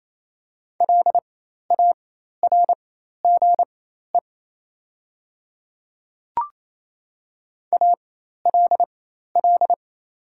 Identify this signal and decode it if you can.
Morse code, heard as a keyed tone, about 28 wpm, spaced as for 13 wpm, 700 Hz. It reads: LARGE ALL
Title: 4x Character Spacing - Top 500 Words - 28wpm